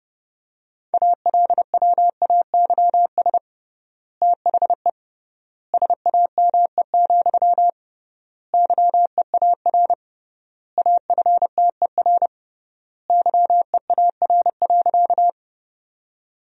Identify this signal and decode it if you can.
Morse code, heard as a keyed tone, about 32 wpm, 700 Hz. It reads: ALWAYS THE SAME, YEAR AFTER YEAR.
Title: 3x Word-Spacing - Sentences from Top 300 Words - 30wpm